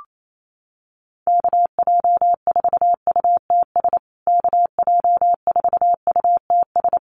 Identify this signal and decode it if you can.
Morse code, heard as a keyed tone, about 28 wpm, 700 Hz. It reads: KJ4UTS KJ4UTS
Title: USA Call Signs Repeated 3x 28wpm